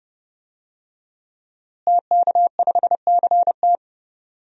Morse code, heard as a keyed tone, about 30 wpm, 700 Hz. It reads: TK5CT
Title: DXCC-100 Call Signs - 5 Characters - Repeated 2x - 30wpm